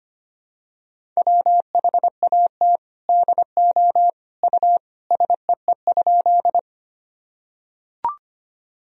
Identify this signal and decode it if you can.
Morse code, heard as a keyed tone, about 25 wpm, 700 Hz. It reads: WHAT DO U SEE?
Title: Sentences from Top 100 Words and Abbreviations 25wpm